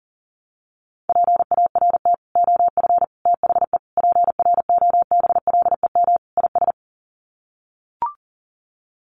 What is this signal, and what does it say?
Morse code, heard as a keyed tone, about 40 wpm, 700 Hz. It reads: PART OF THE PROBLEM IS
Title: Sets of 5 Words Encore2 40wpm